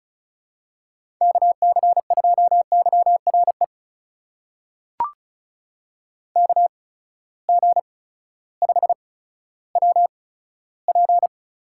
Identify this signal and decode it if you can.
Morse code, heard as a keyed tone, about 36 wpm, spaced as for 16 wpm, 700 Hz. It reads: KC2YRE KG5WP
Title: USA Call Signs - 6 Characters - 8x Character Spacing - 35wpm